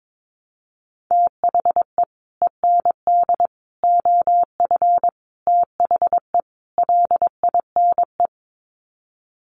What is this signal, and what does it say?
Morse code, heard as a keyed tone, about 22 wpm, 700 Hz. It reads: THE END OF THE LINE
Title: Sets of 5 Words Encore2 22wpm